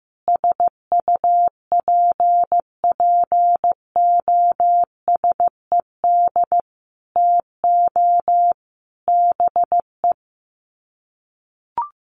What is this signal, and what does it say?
Morse code, heard as a keyed tone, about 15 wpm, 700 Hz. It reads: SUPPOSED TO BE
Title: Sets of 5 Words Encore 15wpm